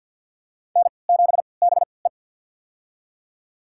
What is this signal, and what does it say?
Morse code, heard as a keyed tone, about 52 wpm, spaced as for 28 wpm, 700 Hz. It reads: N7BE
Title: USA Call Signs - 3x Character Spacing - 50wpm